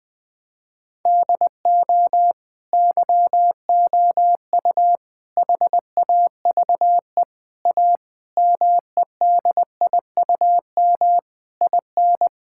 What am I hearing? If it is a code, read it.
Morse code, heard as a keyed tone, about 20 wpm, 700 Hz. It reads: DO YOU HAVE A MEDIUM IN
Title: Sentences from Top 1000 Words - Review Part 6 - 20wpm